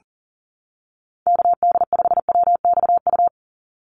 Morse code, heard as a keyed tone, about 40 wpm, 700 Hz. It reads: KD5WXU
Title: USA Call Signs 40wpm